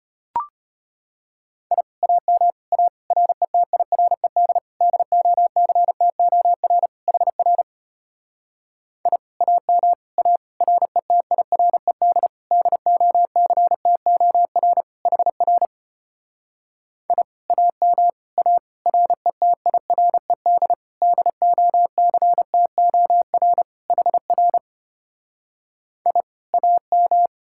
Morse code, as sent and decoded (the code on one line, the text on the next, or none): .. / .- -- / .- / .-. - .. .-. -.. / -.. --- -.-. - --- .-. / .... .-. / .. / .- -- / .- / .-. - .. .-. -.. / -.. --- -.-. - --- .-. / .... .-. / .. / .- -- / .- / .-. - .. .-. -.. / -.. --- -.-. - --- .-. / .... .-. / .. / .- --
I AM A RETIRED DOCTOR HR I AM A RETIRED DOCTOR HR I AM A RETIRED DOCTOR HR I AM